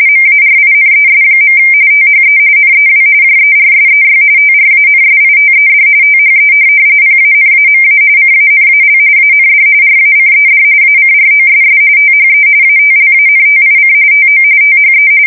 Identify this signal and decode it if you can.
RTTY, about 45.45 baud. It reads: MARINES. THE NEW JERSEY TRANSITED THE PANAMA CANAL AND RACED ACROSS THE ATLANTIC TO THE
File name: fsk170 / USS-New-Jersey